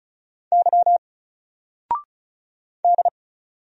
Morse code, heard as a keyed tone, about 36 wpm, 700 Hz. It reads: Y D
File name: Single-Letter-Number-35wpm